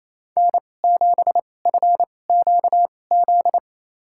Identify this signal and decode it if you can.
Morse code, heard as a keyed tone, about 28 wpm, spaced as for 20 wpm, 700 Hz. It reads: N7FQZ
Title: USA Call Signs - 2x Character Spacing - 28wpm